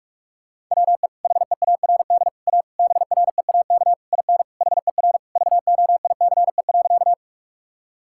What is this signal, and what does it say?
Morse code, heard as a keyed tone, about 41 wpm, 700 Hz. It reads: WE HEARD A BREAK IN HER VOICE.
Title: Sentences from Top 1000 Words - Review Part 6 - 45wpm